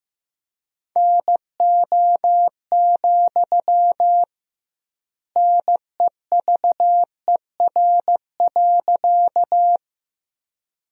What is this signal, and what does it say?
Morse code, heard as a keyed tone, about 16 wpm, 700 Hz. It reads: NO, NEVER.
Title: 2x Word-Spacing - Sentences from Top 300 Words - 15wpm